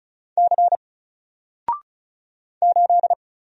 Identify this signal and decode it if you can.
Morse code, heard as a keyed tone, about 36 wpm, 700 Hz. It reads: C 8
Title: Single Character (Letter:Number:Punctuation:Prosign) 35wpm